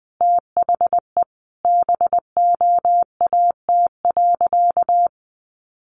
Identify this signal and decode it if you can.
Morse code, heard as a keyed tone, about 20 wpm, 700 Hz. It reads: THE BOAT.
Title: Sentences from Top 1400 Words 20wpm